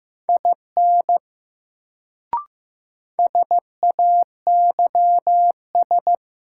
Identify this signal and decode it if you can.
Morse code, heard as a keyed tone, about 15 wpm, 700 Hz. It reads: IN SAYS
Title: Sets of 2 Words - New Word Drill - 15wpm